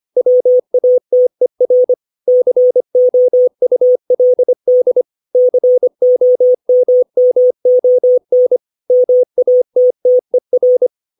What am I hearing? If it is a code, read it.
Morse code, heard as a keyed tone, about 25 wpm, 500 Hz. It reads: WATER COULD COMMON MATTER
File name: 632 words / 632 Words 25WPM